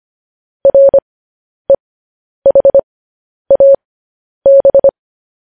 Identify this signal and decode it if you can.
Morse code, heard as a keyed tone, about 25 wpm, spaced as for 10 wpm, 550 Hz. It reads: REHAB
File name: podcast / News headlines-1x-25x10